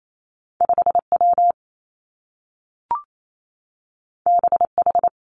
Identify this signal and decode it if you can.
Morse code, heard as a keyed tone, about 28 wpm, 700 Hz. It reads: 5 W B H
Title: Two Challenging Letter-Number Combinations 28wpm